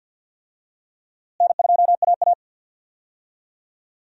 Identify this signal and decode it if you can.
Morse code, heard as a keyed tone, about 51 wpm, 700 Hz. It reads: N2AA
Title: USA Call Signs - 4 Characters - Repeated 3x - 50wpm